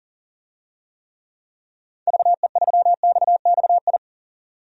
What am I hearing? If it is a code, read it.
Morse code, heard as a keyed tone, about 40 wpm, 700 Hz. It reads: VE3XXI
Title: DXCC-100 Call Signs - 6 Characters - 40wpm